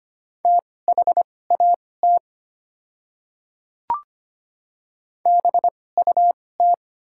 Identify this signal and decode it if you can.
Morse code, heard as a keyed tone, about 25 wpm, spaced as for 18 wpm, 700 Hz. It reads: THAT BUT
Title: 2x Character Spacing - Top 200 Words - 25wpm